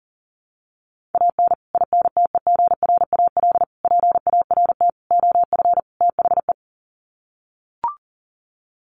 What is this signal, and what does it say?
Morse code, heard as a keyed tone, about 40 wpm, 700 Hz. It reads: AN INTEGRAL PART OF THE
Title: Sets of 5 Words 40wpm